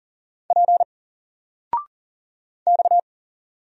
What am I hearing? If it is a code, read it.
Morse code, heard as a keyed tone, about 40 wpm, 700 Hz. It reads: P X
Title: Single Letter-Number 40wpm